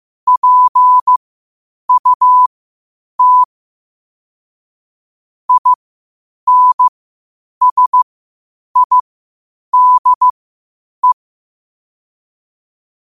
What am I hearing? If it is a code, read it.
Morse code, heard as a keyed tone, about 15 wpm, spaced as for 8 wpm, 1000 Hz. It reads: PUT INSIDE
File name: qotdfiles / 8wpm